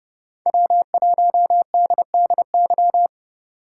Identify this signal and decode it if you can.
Morse code, heard as a keyed tone, about 29 wpm, 700 Hz. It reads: W1DDY